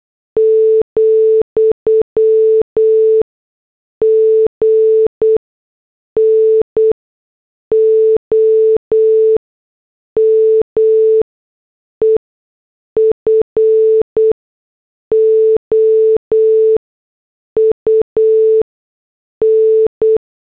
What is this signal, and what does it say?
Morse code, heard as a keyed tone, about 8 wpm, spaced as for 6 wpm, 440 Hz. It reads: , GNOME FOUN
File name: morse / headlines